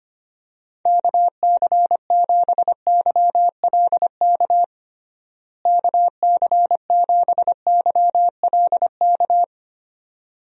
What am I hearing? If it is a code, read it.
Morse code, heard as a keyed tone, about 25 wpm, 700 Hz. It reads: KC7YLK KC7YLK
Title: USA Call Signs - 6 Characters - Repeated 2x - 25wpm